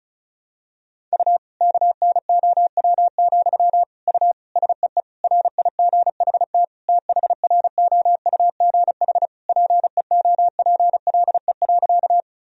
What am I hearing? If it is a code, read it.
Morse code, heard as a keyed tone, about 36 wpm, 700 Hz. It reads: U KNOW, U SEE RIGHT THROUGH PEOPLE.
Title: Sentences from Top 200 Words and Abbreviations 35wpm